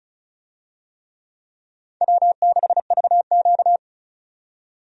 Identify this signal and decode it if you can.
Morse code, heard as a keyed tone, about 34 wpm, 700 Hz. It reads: W6VQ
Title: USA Call Signs - 4 Characters - 35wpm